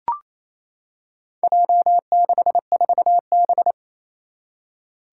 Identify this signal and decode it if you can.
Morse code, heard as a keyed tone, about 28 wpm, 700 Hz. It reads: J64B
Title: Four Challenging Letter-Number Combinations 28wpm